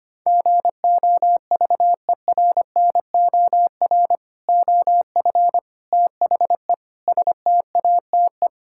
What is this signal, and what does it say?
Morse code, heard as a keyed tone, about 25 wpm, 700 Hz. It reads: GOVERNOR OF THE STATE
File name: practice / 25wpm / Sentences with 3-Word Phrases - 25wpm